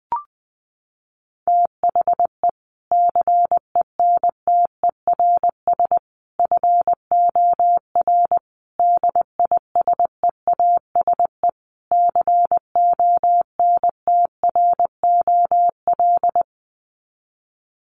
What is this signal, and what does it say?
Morse code, heard as a keyed tone, about 20 wpm, 700 Hz. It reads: THE CENTERS FOR DISEASE CONTROL
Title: Sets of 5 Words Encore2 20wpm